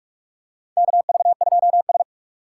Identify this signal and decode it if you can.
Morse code, heard as a keyed tone, about 41 wpm, 700 Hz. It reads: KV2S
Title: USA Call Signs - 4 Characters - 8x Character Spacing - 45wpm